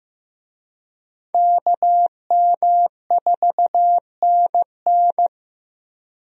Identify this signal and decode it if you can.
Morse code, heard as a keyed tone, about 15 wpm, 700 Hz. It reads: KM4NN
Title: USA Call Signs - 5 Characters - Repeated 2x - 15wpm